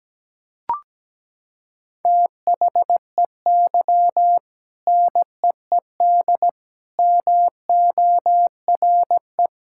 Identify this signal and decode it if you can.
Morse code, heard as a keyed tone, about 17 wpm, 700 Hz. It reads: THEY NEED MORE
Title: Sentences from Top 1000 Words - Review Part 6 - 17wpm